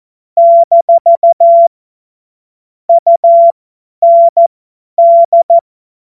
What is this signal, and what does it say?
Morse code, heard as a keyed tone, about 14 wpm, spaced as for 10 wpm, 680 Hz. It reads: - UND
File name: Morsekurs / Witze / Witz 23. Dezember 2025 50 F70